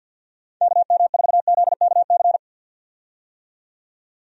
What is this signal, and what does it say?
Morse code, heard as a keyed tone, about 52 wpm, 700 Hz. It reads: KM4ZKC
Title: USA Call Signs - 6 Characters - 50wpm